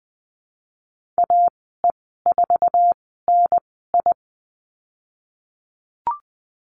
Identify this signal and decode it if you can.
Morse code, heard as a keyed tone, about 20 wpm, spaced as for 15 wpm, 700 Hz. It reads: AE4NI
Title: 055 Lesson 10g - Letter r - Call Signs 20wpm